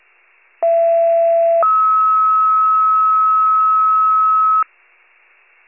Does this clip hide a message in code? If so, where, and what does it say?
Yes: two-tone page from 0.6 to 4.6 s, heard as a one-second tone, then a three-second tone, about 670 then 1290 Hz.